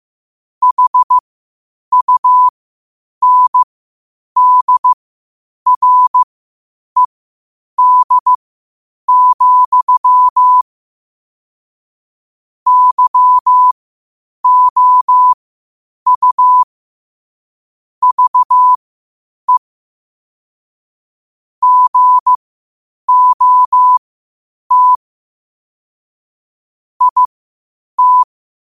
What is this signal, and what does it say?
Morse code, heard as a keyed tone, about 15 wpm, spaced as for 8 wpm, 1000 Hz. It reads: HUNDRED, YOU VE GOT IT
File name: qotdfiles / 8wpm